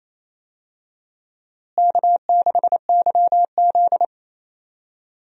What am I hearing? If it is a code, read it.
Morse code, heard as a keyed tone, about 28 wpm, 700 Hz. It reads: K6YZ
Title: USA Call Signs - 4 Characters - Repeated 3x - 28wpm